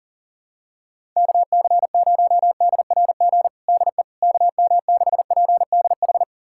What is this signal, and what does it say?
Morse code, heard as a keyed tone, about 40 wpm, 700 Hz. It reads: KC0DRG DE KM6PDH
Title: USA Callsign DE Callsign - 6 Characters - 40wpm